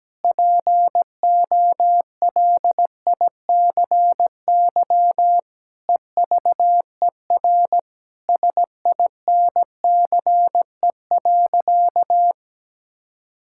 Morse code, heard as a keyed tone, about 17 wpm, 700 Hz. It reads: POLICY EVER SINCE.
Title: Sentences from Top 1000 Words - Review Part 5 - 17wpm